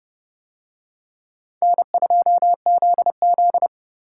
Morse code, heard as a keyed tone, about 26 wpm, 700 Hz. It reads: N2ZZ